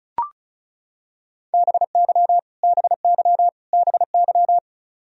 Morse code, heard as a keyed tone, about 34 wpm, 700 Hz. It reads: BY BY BY
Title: Top 100 Words Repeated 35wpm